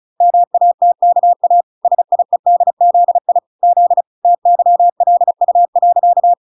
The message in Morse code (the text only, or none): MATKA SIEDZI Z TYLU.